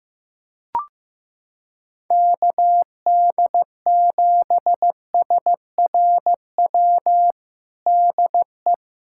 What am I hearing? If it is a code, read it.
Morse code, heard as a keyed tone, about 15 wpm, 700 Hz. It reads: KD7SRW DE
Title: USA Callsign DE Callsign - 6 Characters - 15wpm